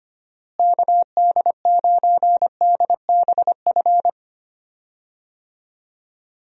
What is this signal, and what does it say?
Morse code, heard as a keyed tone, about 25 wpm, 700 Hz. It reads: KD9DBF
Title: USA Call Signs - 6 Characters - Repeated 2x - 25wpm